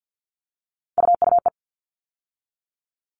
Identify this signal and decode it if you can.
Morse code, heard as a keyed tone, about 50 wpm, 700 Hz. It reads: UUE
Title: All Cut Numbers - 3-Digit 50wpm